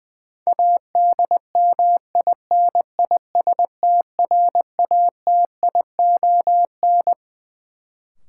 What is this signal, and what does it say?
Morse code, heard as a keyed tone, about 20 wpm, 700 Hz. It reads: ADMINISTRATION
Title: Sets of 5 Words - New Word Drill - 20wpm